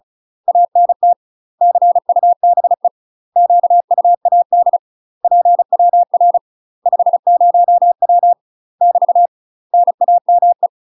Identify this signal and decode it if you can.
Morse code, heard as a keyed tone, about 33 wpm, 700 Hz. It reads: ANT CUBE QUAD PWR 50W <BT> NAME
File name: UK MASTER QSOs TRAINING / 35WPM / UK MASTER QSOs 0013